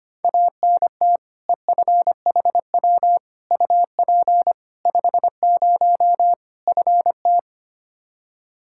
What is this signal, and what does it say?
Morse code, heard as a keyed tone, about 25 wpm, 700 Hz. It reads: ANT EFHW UP 50 FT
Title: Practice QSO - Exchange 2 - Info - 25wpm